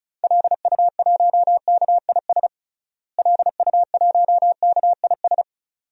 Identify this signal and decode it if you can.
Morse code, heard as a keyed tone, about 34 wpm, 700 Hz. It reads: LU1KIS LU1KIS
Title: DXCC-100 Call Signs - 6 Characters - Repeated 2x - 35wpm